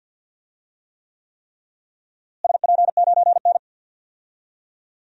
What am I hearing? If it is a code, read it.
Morse code, heard as a keyed tone, about 47 wpm, 700 Hz. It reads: SP9N
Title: DXCC-100 Call Signs - 4 Characters - Repeated 3x - 50wpm